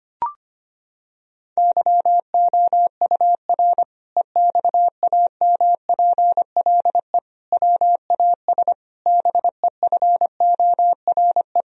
Morse code, heard as a keyed tone, about 25 wpm, 700 Hz. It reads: YOUR EXAMPLE WAS BEFORE
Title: Sentences from Top 1000 Words - Review Part 1 - 25wpm